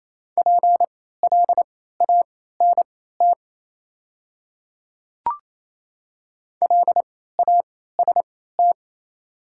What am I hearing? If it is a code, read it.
Morse code, heard as a keyed tone, about 28 wpm, spaced as for 16 wpm, 700 Hz. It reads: PLANT LAST
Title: 3x Character Spacing - Top 500 Words - 28wpm